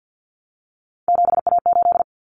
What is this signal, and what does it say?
Morse code, heard as a keyed tone, about 50 wpm, 700 Hz. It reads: 7A8
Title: Common Cut Numbers - 3-Digit 50wpm